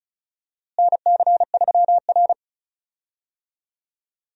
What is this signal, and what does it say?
Morse code, heard as a keyed tone, about 34 wpm, 700 Hz. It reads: NC3R
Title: USA Call Signs - 4 Characters - Repeated 2x - 35wpm